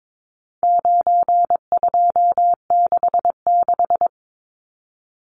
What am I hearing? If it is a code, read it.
Morse code, heard as a keyed tone, about 22 wpm, 700 Hz. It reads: 9266
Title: Numbers - 4-Digit 22wpm